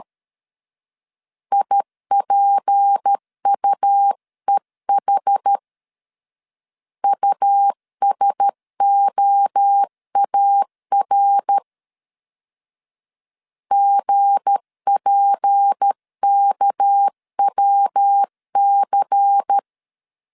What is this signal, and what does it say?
Morse code, heard as a keyed tone, about 13 wpm, 800 Hz. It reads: IPUEH USOAR GPKWC
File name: LEZIONI / 7 GF9Z / 7 GF 12wpm 2122